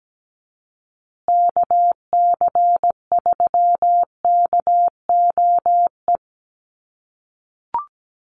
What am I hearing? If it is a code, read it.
Morse code, heard as a keyed tone, about 17 wpm, 700 Hz. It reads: KC3KOE
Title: USA Call Signs Repeated 3x 17wpm